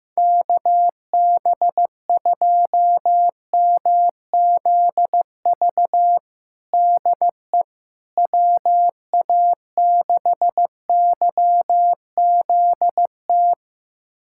Morse code, heard as a keyed tone, about 15 wpm, 700 Hz. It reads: KB2MZV DE WA6YZT
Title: USA Callsign DE Callsign - 6 Characters - 15wpm